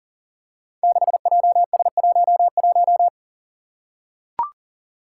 Morse code, heard as a keyed tone, about 40 wpm, 700 Hz. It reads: 6JS11
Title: Five Challenging Letter-Number Combinations 40wpm